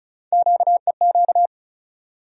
Morse code, heard as a keyed tone, about 34 wpm, 700 Hz. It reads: QEQ
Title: USA Call Sign Postfixes - 3 Letters 35wpm